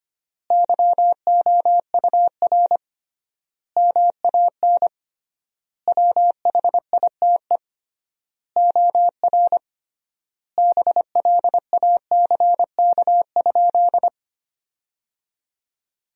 Morse code, heard as a keyed tone, about 25 wpm, 700 Hz. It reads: YOUR MAN WHITE OR BLACK?
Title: 3x Word-Spacing - Sentences from Top 400 Words - 25wpm